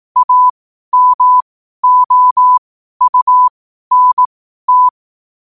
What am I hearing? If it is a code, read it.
Morse code, heard as a keyed tone, about 18 wpm, spaced as for 12 wpm, 1000 Hz. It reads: AMOUNT
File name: qotdfiles / 12wpm